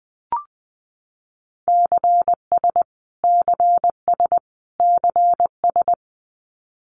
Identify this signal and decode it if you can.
Morse code, heard as a keyed tone, about 20 wpm, 700 Hz. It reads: CS CS CS
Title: Abbreviations Repeated 3x - 20wpm